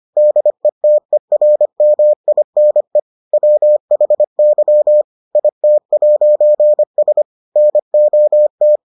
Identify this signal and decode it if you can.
Morse code, heard as a keyed tone, about 25 wpm, 600 Hz. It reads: DETERMINE WHY IT'S NOT